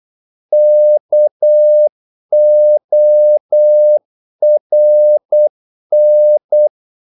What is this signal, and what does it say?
Morse code, heard as a keyed tone, about 8 wpm, 600 Hz. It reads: KORN
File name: Korte ord DK 8wpm